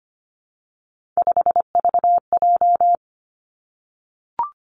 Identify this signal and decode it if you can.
Morse code, heard as a keyed tone, about 25 wpm, 700 Hz. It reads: 5VJ
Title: Three Challenging Letter-Number Combinations 25wpm